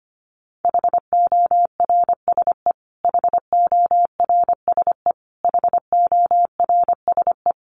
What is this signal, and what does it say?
Morse code, heard as a keyed tone, about 25 wpm, 700 Hz. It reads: HORSE HORSE HORSE
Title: Top 500 Words Repeated 25wpm